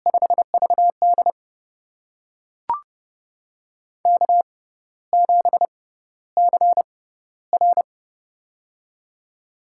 Morse code, heard as a keyed tone, about 30 wpm, spaced as for 14 wpm, 700 Hz. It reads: 5VD K7CR
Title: USA Call Signs - 4 Characters - 6x Character Spacing - 30wpm